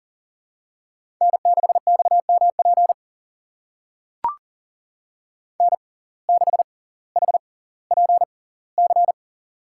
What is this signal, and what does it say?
Morse code, heard as a keyed tone, about 39 wpm, spaced as for 20 wpm, 700 Hz. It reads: N6XMP N6HPC
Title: USA Call Signs - 5 Characters - 6x Character Spacing - 40wpm